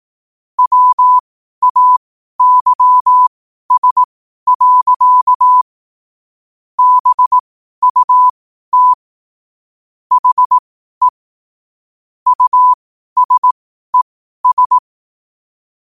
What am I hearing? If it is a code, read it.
Morse code, heard as a keyed tone, about 18 wpm, spaced as for 12 wpm, 1000 Hz. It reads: WAYS. BUT HE USES
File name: qotdfiles / 12wpm